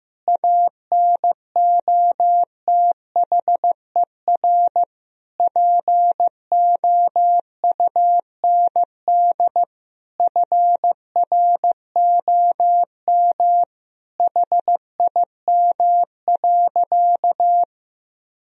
Morse code, heard as a keyed tone, about 15 wpm, 700 Hz. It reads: ANOTHER POUND FROM HIM.
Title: Sentences from Top 1000 Words - Review Part 2 - 15wpm